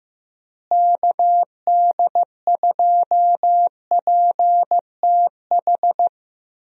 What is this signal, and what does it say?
Morse code, heard as a keyed tone, about 15 wpm, 700 Hz. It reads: KD2PTH
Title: USA Call Signs - 6 Characters - Repeated 3x - 15wpm